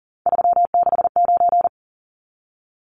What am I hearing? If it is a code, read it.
Morse code, heard as a keyed tone, about 39 wpm, 700 Hz. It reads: 369
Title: Numbers - 3-Digit 40wpm